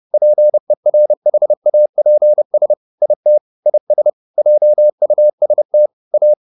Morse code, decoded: PERHAPS IT IS JUST A

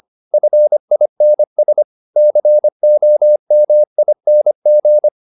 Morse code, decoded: FINS COMING

